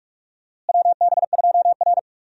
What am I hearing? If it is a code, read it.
Morse code, heard as a keyed tone, about 46 wpm, 700 Hz. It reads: WB2R
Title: USA Call Signs - 4 Characters - Repeated 3x - 45wpm